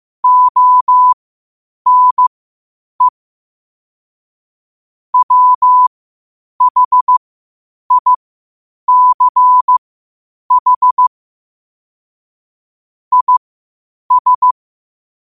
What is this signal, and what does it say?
Morse code, heard as a keyed tone, about 15 wpm, spaced as for 8 wpm, 1000 Hz. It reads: ONE WHICH IS
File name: qotdfiles / 8wpm